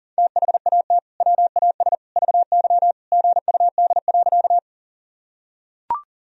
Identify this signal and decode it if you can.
Morse code, heard as a keyed tone, about 40 wpm, 700 Hz. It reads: THAT WAS VY GUD.
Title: Sentences from Top 200 Words and Abbreviations 40wpm